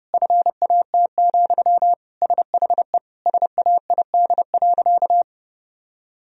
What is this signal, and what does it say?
Morse code, heard as a keyed tone, about 30 wpm, 700 Hz. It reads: FAT, SHE SAID.
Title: Sentences from Top 1000 Words - Review Part 4 - 30wpm